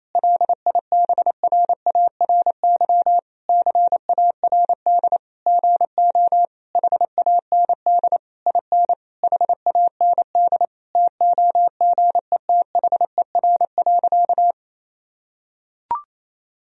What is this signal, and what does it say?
Morse code, heard as a keyed tone, about 28 wpm, 700 Hz. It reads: LIBRARY CARD GO HAND IN HAND TOGETHER.
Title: Sentences from Top 1000 Words - Review Part 4 - 28wpm